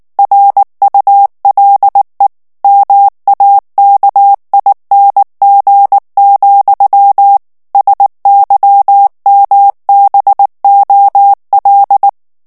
Morse code, heard as a keyed tone, about 19 wpm, 800 Hz. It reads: RULE MAKING, SYMBOL
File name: morse mp3 / ANGLAIS texte clair 3 vitesse 18 mots minute